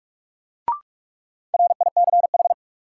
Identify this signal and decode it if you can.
Morse code, heard as a keyed tone, about 45 wpm, 700 Hz. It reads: RICH
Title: Sentences from Top 1100 Words - New Word Drill Alt - 45wpm